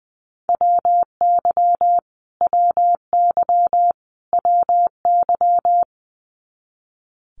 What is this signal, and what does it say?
Morse code, heard as a keyed tone, about 20 wpm, 700 Hz. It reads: WY WY WY
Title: US States Repeated 3x 20wpm